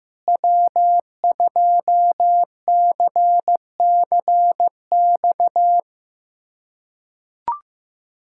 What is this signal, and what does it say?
Morse code, heard as a keyed tone, about 15 wpm, 700 Hz. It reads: W2CCX
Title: USA Callsign DE Callsign - 5 Characters - 15wpm